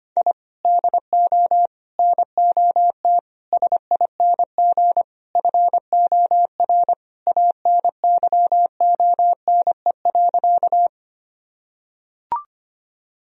Morse code, heard as a keyed tone, about 25 wpm, 700 Hz. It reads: I DO NOT SING FOR ANYONE.
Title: Sentences from Top 1000 Words - Review Part 2 - 25wpm